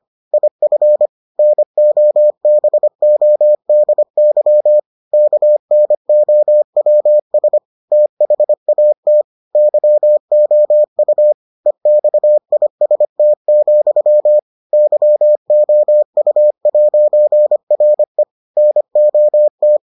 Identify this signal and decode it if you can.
Morse code, heard as a keyed tone, about 25 wpm, 600 Hz. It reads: IF NOBODY KNOWS THAT YOU EXIST, YOU'RE NOT